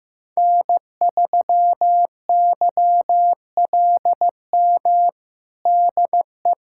Morse code, heard as a keyed tone, about 15 wpm, 700 Hz. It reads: N3YLM DE
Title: USA Callsign DE Callsign - 5 Characters - 15wpm